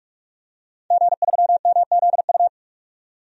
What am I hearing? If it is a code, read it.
Morse code, heard as a keyed tone, about 46 wpm, 700 Hz. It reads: G3MZU
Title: DXCC-100 Call Signs - 5 Characters - 45wpm